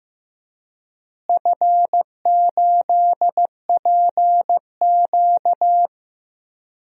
Morse code, heard as a keyed tone, about 15 wpm, 700 Hz. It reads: F8PQ
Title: DXCC-100 Call Signs - 4 Characters - Repeated 3x - 15wpm